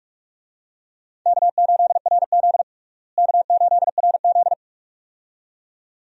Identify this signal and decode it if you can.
Morse code, heard as a keyed tone, about 46 wpm, 700 Hz. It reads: K8RZ K8RZ
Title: USA Call Signs - 4 Characters - Repeated 2x - 45wpm